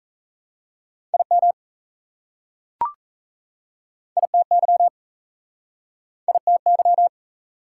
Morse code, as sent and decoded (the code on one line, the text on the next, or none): .. -- / .. - -.-- / .. - -.--
IM ITY ITY